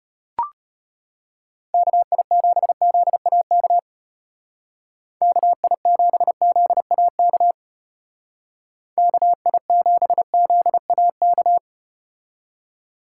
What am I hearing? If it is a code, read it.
Morse code, heard as a keyed tone, about 32 wpm, 700 Hz. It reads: KI7ZAK KI7ZAK KI7ZAK